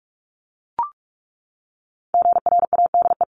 Morse code, heard as a keyed tone, about 44 wpm, 700 Hz. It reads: GRADE
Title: Top 1501-1750 Words 45wpm